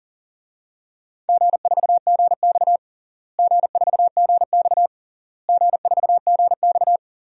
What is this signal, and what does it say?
Morse code, heard as a keyed tone, about 40 wpm, 700 Hz. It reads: G4GX G4GX G4GX
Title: DXCC-100 Call Signs - 4 Characters - Repeated 3x - 40wpm